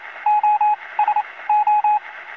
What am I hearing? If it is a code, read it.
Morse code, heard as a keyed tone, about 28 wpm, spaced as for 20 wpm, 840 Hz, with radio noise. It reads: OSO
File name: zvuki-morze-sos